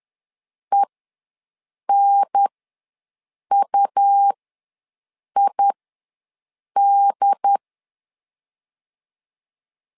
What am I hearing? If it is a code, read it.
Morse code, heard as a keyed tone, about 10 wpm, spaced as for 6 wpm, 800 Hz. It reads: ENUID
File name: LEZIONI / 4 Q ws 10wpm 2306